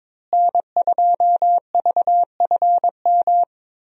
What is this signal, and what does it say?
Morse code, heard as a keyed tone, about 22 wpm, 700 Hz. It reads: N2VFM
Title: USA Call Signs - 5 Characters - Repeated 3x - 22wpm